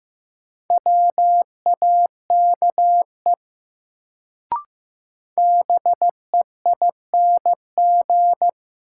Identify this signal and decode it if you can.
Morse code, heard as a keyed tone, about 15 wpm, 700 Hz. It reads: WAKE BEING
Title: Sets of 4 Words - New Word Drill - 15wpm